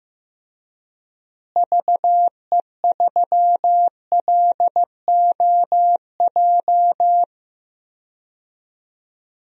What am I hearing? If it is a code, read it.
Morse code, heard as a keyed tone, about 15 wpm, 700 Hz. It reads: VE3LOJ
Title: DXCC-100 Call Signs - 6 Characters - Repeated 3x - 15wpm